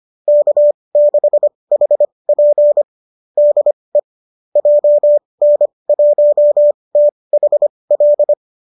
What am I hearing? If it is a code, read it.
Morse code, heard as a keyed tone, about 25 wpm, spaced as for 20 wpm, 570 Hz. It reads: K6HP DE JN1THL
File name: qso205 20